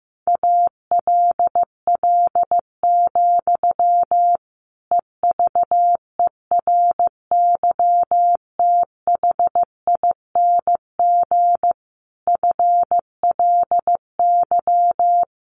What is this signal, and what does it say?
Morse code, heard as a keyed tone, about 15 wpm, 700 Hz. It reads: ALL, EVERYTHING FLY
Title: A Sound Of Thunder 15wpm